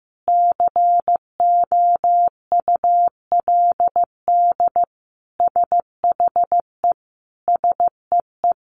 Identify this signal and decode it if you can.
Morse code, heard as a keyed tone, about 15 wpm, 700 Hz. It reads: COULD SHE SEE
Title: Sentences from Top 100 Words 15wpm